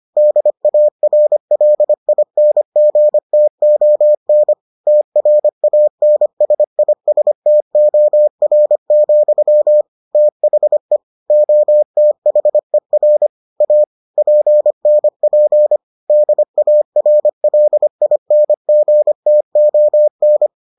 Morse code, heard as a keyed tone, about 25 wpm, 600 Hz. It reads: DARLINGTON TRANSISTOR, THE OTHER A PNP DARLINGTON